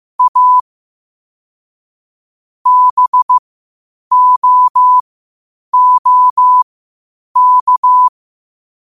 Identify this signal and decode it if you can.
Morse code, heard as a keyed tone, about 15 wpm, spaced as for 8 wpm, 1000 Hz. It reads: A BOOK
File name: qotdfiles / 8wpm